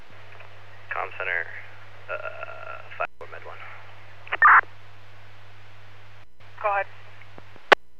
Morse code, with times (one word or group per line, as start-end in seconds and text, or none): none